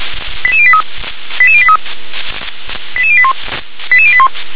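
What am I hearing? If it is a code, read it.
Five-tone selective call (ZVEI), heard as quick runs of five tones, tones of about 70 ms. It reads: 88083 88083 88081 88081
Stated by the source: Received on SDR; ZVEI five-tone selective call.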